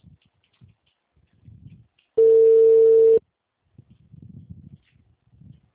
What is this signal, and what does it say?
Ringback tone (China, 450 Hz).
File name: out- -2018- - - .48306